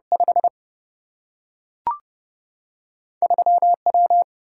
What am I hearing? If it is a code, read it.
Morse code, heard as a keyed tone, about 30 wpm, 700 Hz. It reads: 5 3 W
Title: Single-Letter Single-Number 30wpm